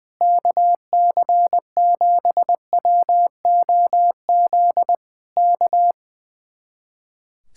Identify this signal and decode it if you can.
Morse code, heard as a keyed tone, about 20 wpm, 700 Hz. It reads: KC7WOZ K
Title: Speed-Racing Common QSO Phrases - 20wpm